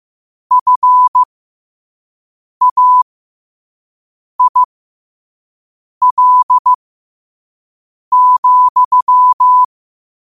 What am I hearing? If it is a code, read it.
Morse code, heard as a keyed tone, about 15 wpm, spaced as for 5 wpm, 1000 Hz. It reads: FAIL,